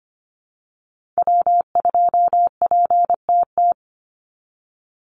Morse code, heard as a keyed tone, about 25 wpm, 700 Hz. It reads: W2PTT
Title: Call Signs Repeated 2x - 25wpm